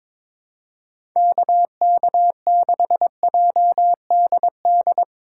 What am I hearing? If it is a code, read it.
Morse code, heard as a keyed tone, about 22 wpm, 700 Hz. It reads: KK6JDD
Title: USA Call Signs - 6 Characters - 4x Character Spacing - 22wpm